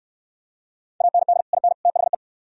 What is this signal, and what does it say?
Morse code, heard as a keyed tone, about 65 wpm, 700 Hz. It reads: AND IN THE